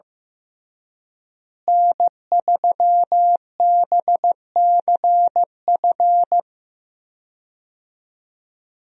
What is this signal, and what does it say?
Morse code, heard as a keyed tone, about 15 wpm, 700 Hz. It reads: N3BCF
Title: USA Call Signs - 5 Characters - Repeated 3x - 15wpm